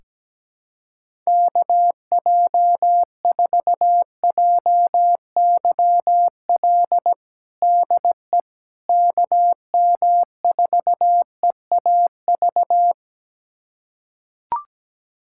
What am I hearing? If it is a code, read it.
Morse code, heard as a keyed tone, about 17 wpm, 700 Hz. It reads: KJ4JYL DE KM4EAV
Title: USA Callsign DE Callsign 17wpm